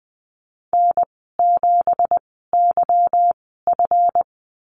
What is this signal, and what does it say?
Morse code, heard as a keyed tone, about 20 wpm, spaced as for 15 wpm, 700 Hz. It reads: N7YF
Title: 173 Lesson 28g - Number 9 - Call Signs 20wpm